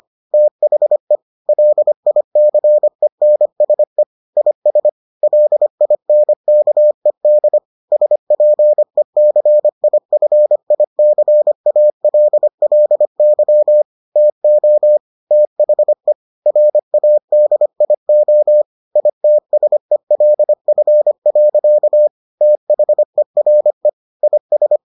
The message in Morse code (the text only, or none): THE LICENSE IS LINKED SPECIFICALLY TO THE RADIO ITSELF. THERE IS